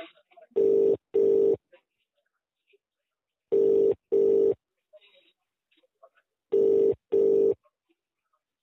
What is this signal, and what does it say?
Ringback tone (UK), 400 + 450 Hz, with speech.